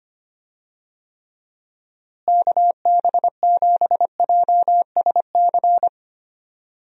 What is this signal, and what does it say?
Morse code, heard as a keyed tone, about 25 wpm, 700 Hz. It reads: KB7JSC
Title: USA Call Signs - 6 Characters - Repeated 2x - 25wpm